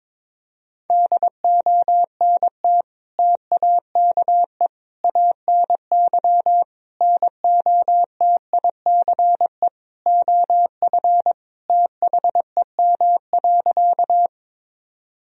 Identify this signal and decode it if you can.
Morse code, heard as a keyed tone, about 22 wpm, 700 Hz. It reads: DONT TAKE ANY NOTICE OF THEM.
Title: Sentences from Top 1000 Words - Review Part 2 - 22wpm